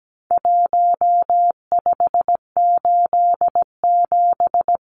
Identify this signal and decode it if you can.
Morse code, heard as a keyed tone, about 17 wpm, 700 Hz. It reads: 1587
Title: Numbers - Contest Style - 4-Digit 17wpm